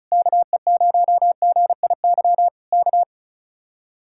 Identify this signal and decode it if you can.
Morse code, heard as a keyed tone, about 32 wpm, 700 Hz. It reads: KE0GIY K
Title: Speed-Racing Common QSO Phrases - 35wpm